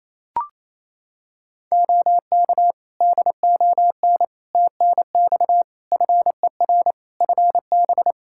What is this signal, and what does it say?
Morse code, heard as a keyed tone, about 27 wpm, 700 Hz. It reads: OK DON TNX FER FB
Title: Practice QSO - Exchange 4 - Ending - 28wpm